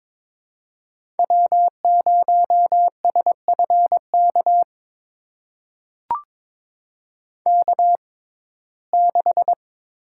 Morse code, heard as a keyed tone, about 22 wpm, spaced as for 16 wpm, 700 Hz. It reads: W0SFK K6
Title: USA Call Signs - 5 Characters - 6x Character Spacing - 22wpm